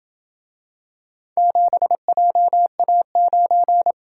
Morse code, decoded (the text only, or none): 7JA9